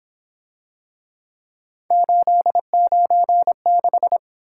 Morse code, heard as a keyed tone, about 26 wpm, 700 Hz. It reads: 896
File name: Speed-Racing - Three Letter-Number Combinations - 22wpm